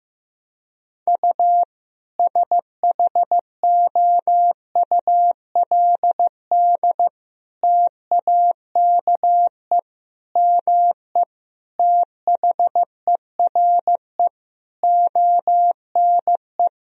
Morse code, heard as a keyed tone, about 15 wpm, 700 Hz. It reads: U SHOULD TAKE ME THERE ONE
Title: Sentences from Top 300 Words and Abbreviations 15wpm